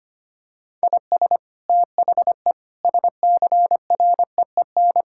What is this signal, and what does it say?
Morse code, heard as a keyed tone, about 25 wpm, 700 Hz. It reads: IS THE SCREEN
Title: Sentences from Top 1000 Words - Review Part 6 - 25wpm